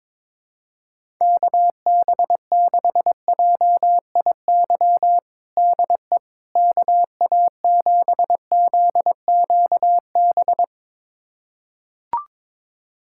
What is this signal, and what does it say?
Morse code, heard as a keyed tone, about 22 wpm, 700 Hz. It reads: KB6JIY DE KA7ZQB
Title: USA Callsign DE Callsign - 6 Characters - 22wpm